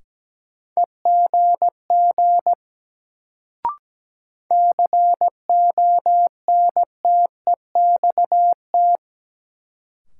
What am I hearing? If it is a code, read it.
Morse code, heard as a keyed tone, about 17 wpm, 700 Hz. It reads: EGG CONTEXT
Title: Sentences from Top 1000 Words - Review Part 7 - New Word Drill - 17wpm